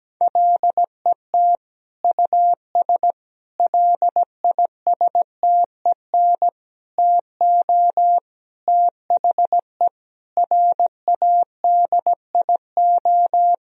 Morse code, heard as a keyed tone, about 17 wpm, 700 Hz. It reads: LET US LISTEN TO THE RADIO
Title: Sentences from Top 1000 Words - Review Part 3 - 17wpm